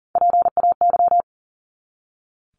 Morse code, heard as a keyed tone, about 40 wpm, 700 Hz. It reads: PAY
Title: Top 501-750 Words 40wpm